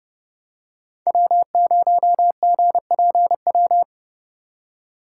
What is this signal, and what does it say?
Morse code, heard as a keyed tone, about 29 wpm, 700 Hz. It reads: W0GPW